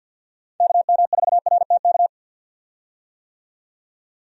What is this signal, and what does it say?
Morse code, heard as a keyed tone, about 52 wpm, 700 Hz. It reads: KM4RTK
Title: USA Call Signs - 6 Characters - 50wpm